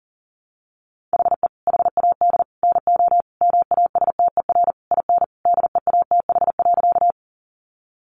Morse code, heard as a keyed tone, about 40 wpm, 700 Hz. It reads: HE HAD NO MASTER IN DEATH.
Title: Sentences from Top 1300 Words 40wpm